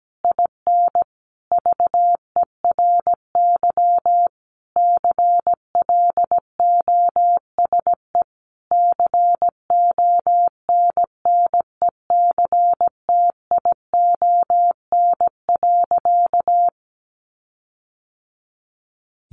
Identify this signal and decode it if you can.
Morse code, heard as a keyed tone, about 17 wpm, 700 Hz. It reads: IN VERY CLOSE CONNECTION.
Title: Sentences from Top 1300 Words 17wpm